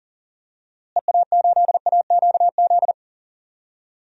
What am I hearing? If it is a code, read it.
Morse code, heard as a keyed tone, about 39 wpm, 700 Hz. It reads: EA8AQZ
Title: DXCC-100 Call Signs - 6 Characters - Repeated 2x - 40wpm